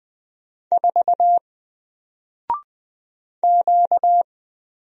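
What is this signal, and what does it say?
Morse code, heard as a keyed tone, about 20 wpm, 700 Hz. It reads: 4 Q
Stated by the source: Single letter and number Morse drill at 20 wpm, from N4LCD.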